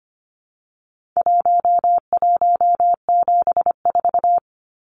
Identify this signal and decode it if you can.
Morse code, heard as a keyed tone, about 25 wpm, 700 Hz. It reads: 1174
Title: Numbers - Contest Style - 4-Digit 25wpm